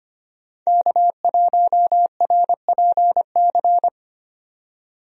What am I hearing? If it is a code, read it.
Morse code, heard as a keyed tone, about 25 wpm, 690 Hz. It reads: K1RPC
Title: USA Call Signs - 5 Characters - 25wpm